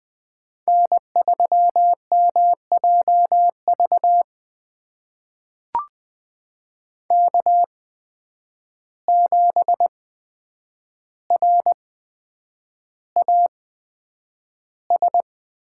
Morse code, heard as a keyed tone, about 20 wpm, spaced as for 9 wpm, 700 Hz. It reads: N3MJV K7RAS
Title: USA Call Signs - 5 Characters - 8x Character Spacing - 20wpm